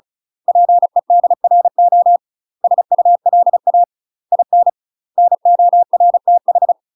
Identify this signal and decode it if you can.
Morse code, heard as a keyed tone, about 34 wpm, 700 Hz. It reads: PEDRO SULA IN NORTH
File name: UK MASTER QSOs TRAINING/35WPM